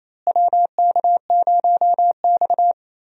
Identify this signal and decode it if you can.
Morse code, heard as a keyed tone, about 28 wpm, 700 Hz. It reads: WK0X
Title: USA Call Signs - 4 Characters - Repeated 2x - 28wpm